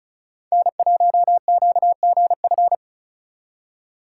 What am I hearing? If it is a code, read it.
Morse code, heard as a keyed tone, about 42 wpm, 700 Hz. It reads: N1QGF
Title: Speed Racing - USA 1x3 Call Signs 35wpm